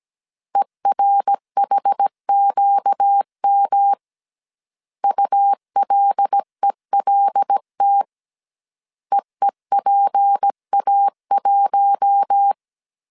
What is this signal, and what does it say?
Morse code, heard as a keyed tone, about 17 wpm, 800 Hz. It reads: ERHQM ULELT EEPA1
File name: CW / DET / 1 6 1500C S A 16wpm 2635